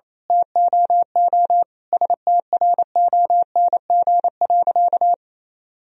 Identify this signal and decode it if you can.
Morse code, heard as a keyed tone, about 28 wpm, 700 Hz. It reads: TOO STRONG.
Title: Sentences from Top 1000 Words - Review Part 4 - 28wpm